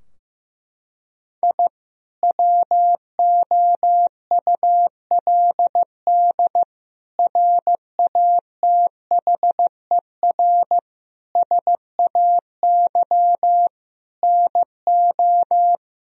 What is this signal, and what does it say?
Morse code, heard as a keyed tone, about 15 wpm, 700 Hz. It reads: I WOULD RATHER SAY NO
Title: Sentences from Top 1000 Words - Review Part 4 - 15wpm